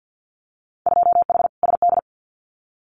Morse code, heard as a keyed tone, about 50 wpm, 700 Hz. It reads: 2H SD
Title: Winter Field Day Exchanges 50wpm